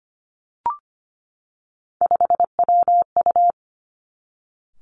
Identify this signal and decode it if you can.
Morse code, heard as a keyed tone, about 25 wpm, 700 Hz. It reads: 5WU
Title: Three Challenging Letter-Number Combinations 25wpm